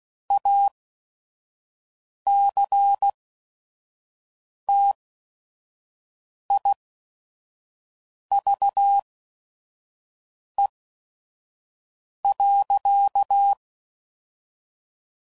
Morse code, heard as a keyed tone, about 16 wpm, spaced as for 5 wpm, 800 Hz. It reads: ACTIVE.
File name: CW Code Groups @5WPM / Technical Subjects / OR Gate